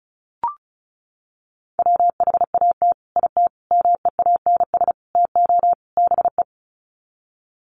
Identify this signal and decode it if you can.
Morse code, heard as a keyed tone, about 34 wpm, 700 Hz. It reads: WHAT IT MEANS TO BE
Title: Sets of 5 Words 35wpm